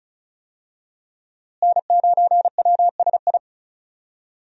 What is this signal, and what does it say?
Morse code, heard as a keyed tone, about 34 wpm, 700 Hz. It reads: N9WSI
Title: USA Call Signs - 5 Characters - Repeated 3x - 35wpm